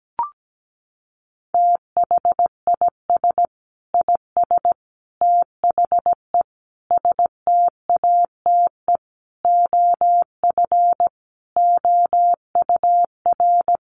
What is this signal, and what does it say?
Morse code, heard as a keyed tone, about 17 wpm, 700 Hz. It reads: THIS IS THE STATE OF OUR
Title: Sentences from Top 1400 Words 17wpm